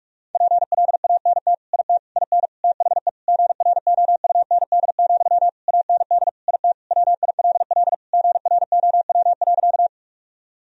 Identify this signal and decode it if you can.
Morse code, heard as a keyed tone, about 41 wpm, 700 Hz. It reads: PLANT IT IN THE GROUND, AND IT WILL GROW.
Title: Sentences from Top 1000 Words - Review Part 2 - 45wpm